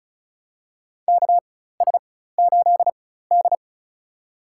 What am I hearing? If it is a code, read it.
Morse code, heard as a keyed tone, about 36 wpm, spaced as for 16 wpm, 700 Hz. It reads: KS8D
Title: USA Call Signs - 4x Character Spacing - 35wpm